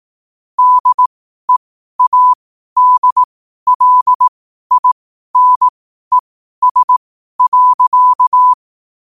Morse code, heard as a keyed tone, about 18 wpm, spaced as for 12 wpm, 1000 Hz. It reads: DEADLINES.